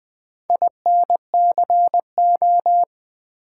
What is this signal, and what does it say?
Morse code, heard as a keyed tone, about 20 wpm, 700 Hz. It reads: INCO
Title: Build A Word - 4 Letters At A Time - 11-Letter Words - 20wpm